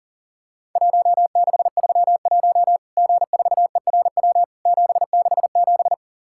Morse code, heard as a keyed tone, about 42 wpm, 700 Hz. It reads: 1631 G4ERW 767
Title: Worked All Europe DX Contest - QTC Exchange - 40wpm